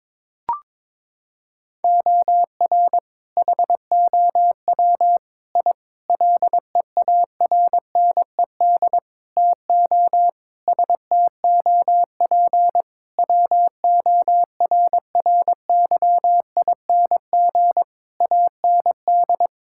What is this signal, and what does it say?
Morse code, heard as a keyed tone, about 21 wpm, 700 Hz. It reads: OR HOW I LEARNED TO STOP WORRYING AND
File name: practice / 22wpm / Sentences with 8-Word Phrases - 22wpm